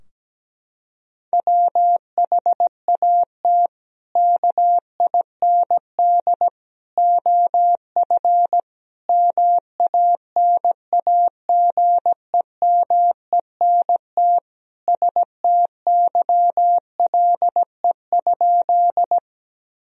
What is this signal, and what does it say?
Morse code, heard as a keyed tone, about 17 wpm, 700 Hz. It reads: WHAT KIND OF MANAGEMENT STYLE?
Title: Sentences from Top 1000 Words - Review Part 4 - 17wpm